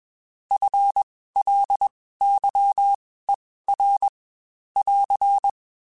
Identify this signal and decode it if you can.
Morse code, heard as a keyed tone, about 21 wpm, spaced as for 15 wpm, 800 Hz. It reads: FLYER <AR>
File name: Spaced / FL Runs sp mp3s / FL Runs sp @20